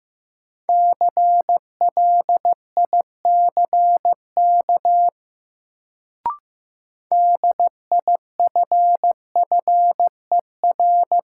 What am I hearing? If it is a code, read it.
Morse code, heard as a keyed tone, about 15 wpm, 700 Hz. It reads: CLICK DIFFER
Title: Sentences from Top 1100 Words - New Word Drill Alt - 15wpm